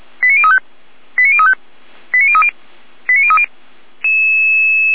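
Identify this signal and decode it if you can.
Five-tone selective call (ZVEI), heard as quick runs of five tones, tones of about 70 ms. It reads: 89035 89035 89030 89030
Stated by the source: ZVEI five-tone call 89502 received by SDR.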